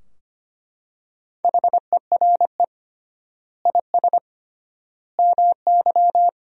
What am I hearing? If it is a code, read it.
Morse code, heard as a keyed tone, about 25 wpm, 700 Hz. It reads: HERE IS MY
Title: 3x Word-Spacing - Sentences from Top 200 Words - 25wpm